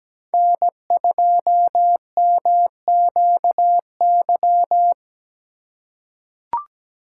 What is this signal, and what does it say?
Morse code, heard as a keyed tone, about 17 wpm, 700 Hz. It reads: N2MQY
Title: USA Call Signs - 3x Character Spacing - 17wpm